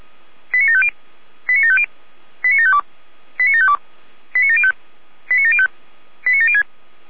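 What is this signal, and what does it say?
Five-tone selective call (ZVEI), heard as quick runs of five tones, tones of about 70 ms. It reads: 89750 89750 89752 89752 89705 89705 89706
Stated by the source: Five-tone ZVEI selective call received on SDR.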